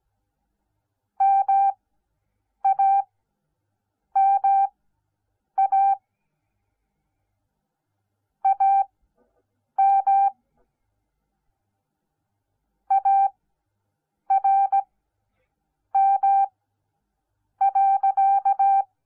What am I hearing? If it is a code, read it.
Morse code, heard as a keyed tone, about 17 wpm, spaced as for 7 wpm, 780 Hz. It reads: MAMA AM ARM.